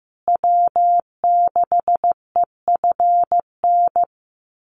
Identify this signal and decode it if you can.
Morse code, heard as a keyed tone, about 15 wpm, 700 Hz. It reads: W6EFN
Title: USA Call Signs Repeated 3x 15wpm